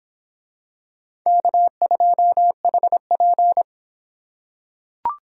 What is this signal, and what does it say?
Morse code, heard as a keyed tone, about 26 wpm, 700 Hz. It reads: K2HP